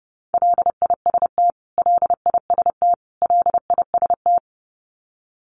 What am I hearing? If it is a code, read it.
Morse code, heard as a keyed tone, about 30 wpm, 700 Hz. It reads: LIST LIST LIST
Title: Top 500 Words Repeated 30wpm